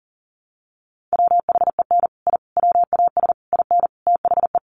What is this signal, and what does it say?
Morse code, heard as a keyed tone, about 40 wpm, 700 Hz. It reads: WHEN I WAS IN THE
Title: Sets of 5 Words Encore2 40wpm